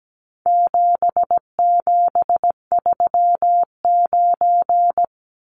Morse code, heard as a keyed tone, about 17 wpm, 700 Hz. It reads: 7739
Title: Numbers - 4-Digit 17wpm